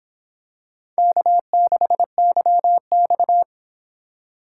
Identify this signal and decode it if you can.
Morse code, heard as a keyed tone, about 26 wpm, 700 Hz. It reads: K6YX